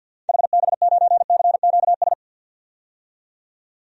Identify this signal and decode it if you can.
Morse code, heard as a keyed tone, about 52 wpm, 700 Hz. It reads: HB9CQS
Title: DXCC-100 Call Signs - 6 Characters - 50wpm